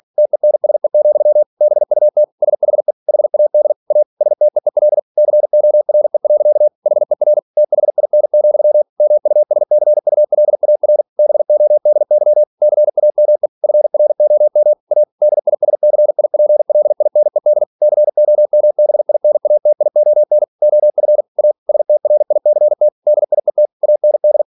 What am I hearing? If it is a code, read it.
Morse code, heard as a keyed tone, about 46 wpm, 600 Hz. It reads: TENSE, BUT SHE HAD A STEEL CORE. HER THIN, MUSCULAR BODY CAME FROM A DISCIPLINED COMBINATION OF A STRICT DIET AND
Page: Never 47wpm